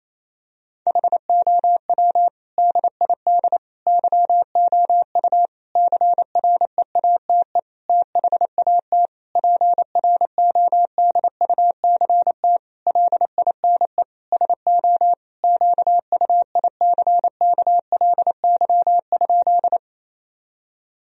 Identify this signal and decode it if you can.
Morse code, heard as a keyed tone, about 28 wpm, 700 Hz. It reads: HOW DID YOU CREATE THAT PRODUCT LINE SO QUICKLY?
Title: Sentences from Top 1000 Words - Review Part 3 - 28wpm